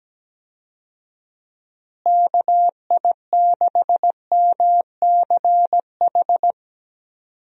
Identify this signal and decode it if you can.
Morse code, heard as a keyed tone, about 17 wpm, 700 Hz. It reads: KI6MCH
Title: USA Call Signs - 6 Characters - Repeated 2x - 17wpm